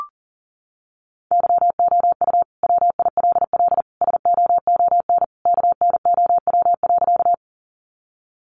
Morse code, heard as a keyed tone, about 40 wpm, 700 Hz. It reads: YOU WILL SOON KNOW.
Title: Sentences from Top 400 Words 40wpm